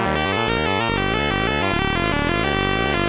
There is no code.